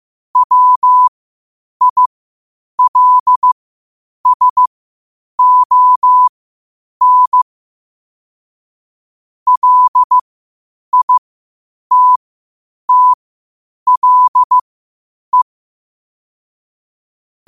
Morse code, heard as a keyed tone, about 15 wpm, spaced as for 8 wpm, 1000 Hz. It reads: WILSON LITTLE
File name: qotdfiles / 8wpm